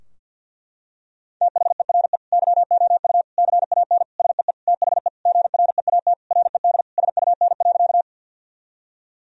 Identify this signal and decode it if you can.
Morse code, heard as a keyed tone, about 50 wpm, 700 Hz. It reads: THERE YOU CAN SEE THE GREAT RED SUN.
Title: Sentences from Top 1000 Words - Review Part 2 - 50wpm